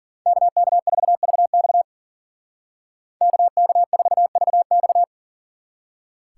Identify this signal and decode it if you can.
Morse code, heard as a keyed tone, about 52 wpm, 700 Hz. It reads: KK4VX KK4VX